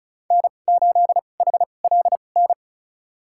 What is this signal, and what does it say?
Morse code, heard as a keyed tone, about 34 wpm, spaced as for 25 wpm, 700 Hz. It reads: N8HLN
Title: USA Call Signs - 5 Characters - 2x Character Spacing - 35wpm